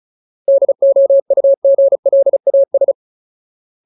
Morse code, heard as a keyed tone, about 34 wpm, 550 Hz. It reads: DOUGLAS
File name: Ultimate Word List / 35WPM Ultimate